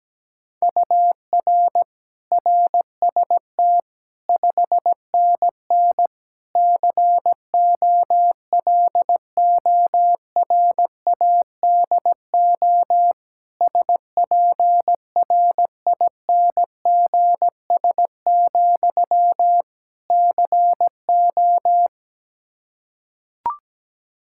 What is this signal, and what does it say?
Morse code, heard as a keyed tone, about 17 wpm, 700 Hz. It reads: UR RST 5NN COLORADO SPRINGS, CO
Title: Practice QSO - Exchange 1 - Beginning - 17wpm